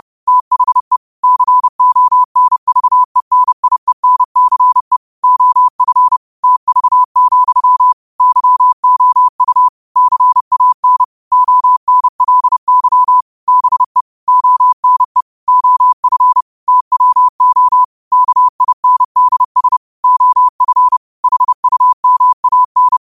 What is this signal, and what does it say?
Morse code, heard as a keyed tone, about 30 wpm, 1000 Hz. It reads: THE CONVENIENCE OF TV, YOU CAN ONLY BE ONE OF TWO KINDS OF HUMAN
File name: qotdfiles / 30wpm